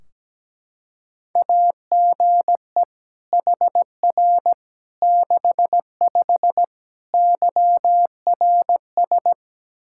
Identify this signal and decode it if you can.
Morse code, heard as a keyed tone, about 17 wpm, 700 Hz. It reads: AGE HR 65 YRS
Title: Practice QSO - Exchange 3 - More Info - 17wpm